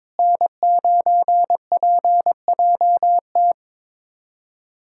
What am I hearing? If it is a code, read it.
Morse code, heard as a keyed tone, about 22 wpm, 700 Hz. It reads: N9PJT
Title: USA Callsign DE Callsign - 5 Characters - 22wpm